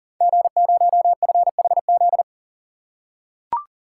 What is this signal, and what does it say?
Morse code, heard as a keyed tone, about 38 wpm, 700 Hz. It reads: G0FHZ
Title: Code Groups - 5 Letter:Numbers - 40wpm